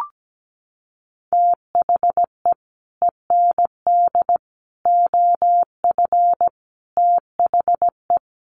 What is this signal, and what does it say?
Morse code, heard as a keyed tone, about 17 wpm, 700 Hz. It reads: THE END OF THE
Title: Sets of 5 Words 17wpm